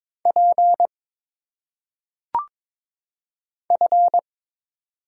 Morse code, heard as a keyed tone, about 22 wpm, 700 Hz. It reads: P F
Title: Single Letters - Phonetic Pronunciation 22wpm